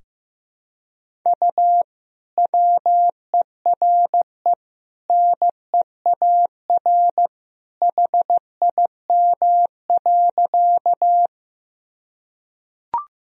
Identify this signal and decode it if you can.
Morse code, heard as a keyed tone, about 15 wpm, 700 Hz. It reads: U WERE NEAR HIM.
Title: Sentences from Top 200 Words and Abbreviations 15wpm